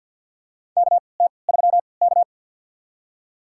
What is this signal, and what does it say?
Morse code, heard as a keyed tone, about 53 wpm, spaced as for 29 wpm, 700 Hz. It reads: KT3K
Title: USA Call Signs - 4 Characters - 3x Character Spacing - 50wpm